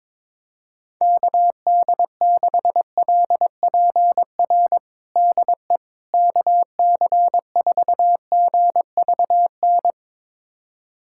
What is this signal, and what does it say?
Morse code, heard as a keyed tone, about 22 wpm, 700 Hz. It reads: KD6LPR DE KC4GVN
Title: USA Callsign DE Callsign - 6 Characters - 22wpm